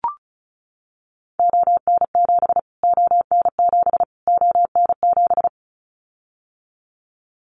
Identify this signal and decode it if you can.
Morse code, heard as a keyed tone, about 34 wpm, 700 Hz. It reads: ON7 ON7 ON7
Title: Callsign Prefixes for Top 25 DXCC - Repeated 3x - 35wpm